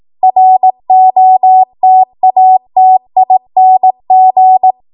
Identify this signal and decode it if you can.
Morse code, heard as a keyed tone, about 18 wpm, 750 Hz. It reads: ROTATING
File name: Morse / 18WPM